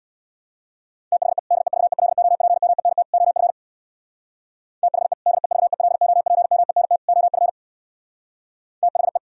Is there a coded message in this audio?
Yes: Morse code from 1.1 to 9.2 s, heard as a keyed tone, about 64 wpm, 700 Hz. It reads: THE DEVELOPMENT OF THE DEVELOPMENT OF THE